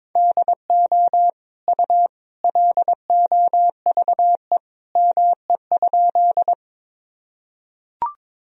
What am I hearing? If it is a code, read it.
Morse code, heard as a keyed tone, about 23 wpm, 700 Hz. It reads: DO U LOVE ME?
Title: Sentences from Top 400 Words and Abbreviations 22wpm